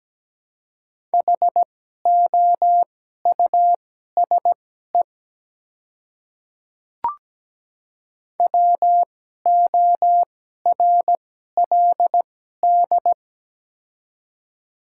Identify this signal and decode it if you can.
Morse code, heard as a keyed tone, about 17 wpm, spaced as for 12 wpm, 700 Hz. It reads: HOUSE WORLD
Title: 2x Character Spacing - Top 200 Words - 17wpm